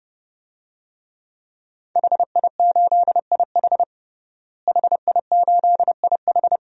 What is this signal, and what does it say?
Morse code, heard as a keyed tone, about 32 wpm, 700 Hz. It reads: HI8IH HI8IH
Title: DXCC-100 Call Signs - 5 Characters - Repeated 2x - 30wpm